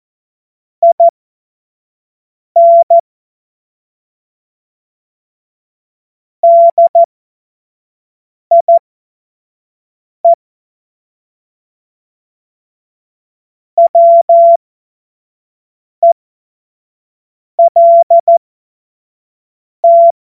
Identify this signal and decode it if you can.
Morse code, heard as a keyed tone, about 14 wpm, spaced as for 5 wpm, 680 Hz. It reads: IN DIE WELT
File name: Morsekurs / Witze / Witz 22. Februar 2022 25 F70